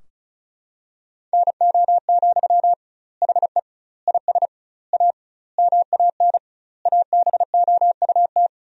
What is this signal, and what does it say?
Morse code, heard as a keyed tone, about 34 wpm, 700 Hz. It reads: NO, HE IS A MAN ABOUT
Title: 2x Word-Spacing - Sentences from Top 400 Words - 35wpm